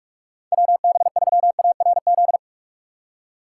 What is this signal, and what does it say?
Morse code, heard as a keyed tone, about 41 wpm, 700 Hz. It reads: WB3ARZ
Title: USA Call Signs - 6 Characters - 2x Character Spacing - 45wpm